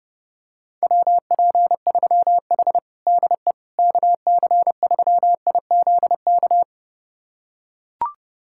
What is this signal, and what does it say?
Morse code, heard as a keyed tone, about 30 wpm, 700 Hz. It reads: WP3H DE KC3IZK
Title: USA Callsign DE Callsign 30wpm